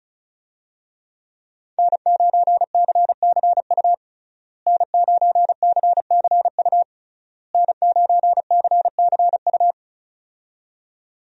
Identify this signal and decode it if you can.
Morse code, heard as a keyed tone, about 36 wpm, 700 Hz. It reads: N9CCU N9CCU N9CCU
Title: USA Call Signs - 5 Characters - Repeated 3x - 35wpm